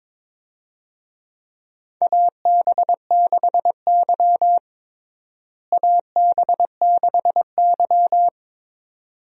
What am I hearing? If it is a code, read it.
Morse code, heard as a keyed tone, about 21 wpm, 700 Hz. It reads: AB6Y AB6Y
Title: USA Call Signs - 4 Characters - Repeated 2x - 22wpm